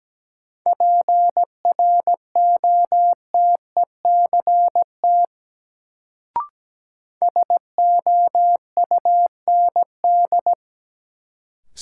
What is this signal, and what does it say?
Morse code, heard as a keyed tone, about 17 wpm, 700 Hz. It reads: PROTECT SOUND
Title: Sentences from Top 900 Words - New Word Drill - 17wpm